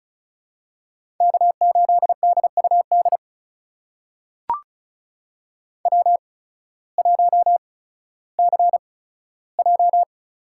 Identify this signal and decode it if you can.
Morse code, heard as a keyed tone, about 34 wpm, spaced as for 16 wpm, 700 Hz. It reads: K8DUD W1CJ
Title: USA Call Signs - 5 Characters - 8x Character Spacing - 35wpm